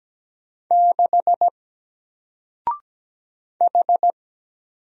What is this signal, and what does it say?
Morse code, heard as a keyed tone, about 17 wpm, 700 Hz. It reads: 6 H
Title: Single Character (Letter:Number:Punctuation:Prosign) 17wpm